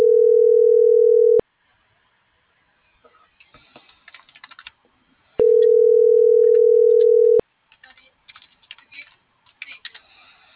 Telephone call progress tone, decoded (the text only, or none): ringback tone